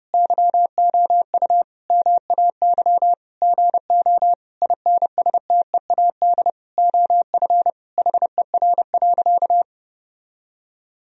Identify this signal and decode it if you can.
Morse code, heard as a keyed tone, about 30 wpm, 700 Hz. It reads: YOU MAY GO INSTEAD OF HER.
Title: Sentences from Top 1000 Words - Review Part 4 - 30wpm